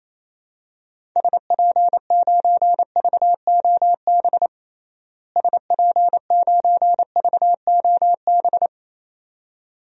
Morse code, heard as a keyed tone, about 28 wpm, 700 Hz. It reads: SP9VOB SP9VOB
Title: DXCC-100 Call Signs - 6 Characters - Repeated 2x - 28wpm